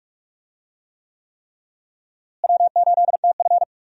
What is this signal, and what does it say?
Morse code, heard as a keyed tone, about 46 wpm, 700 Hz. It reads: W8TF
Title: USA Call Signs - 4 Characters - Repeated 3x - 45wpm